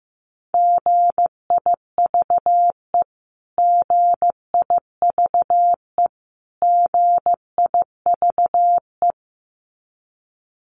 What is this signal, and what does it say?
Morse code, heard as a keyed tone, about 15 wpm, 700 Hz. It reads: GIVE GIVE GIVE
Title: Top 200 Words Repeated 15wpm